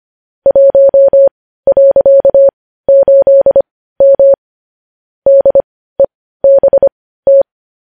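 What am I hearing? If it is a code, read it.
Morse code, heard as a keyed tone, about 25 wpm, spaced as for 15 wpm, 550 Hz. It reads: 1.8M DEBT
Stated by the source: Morse code sending of news headlines at 15 wpm.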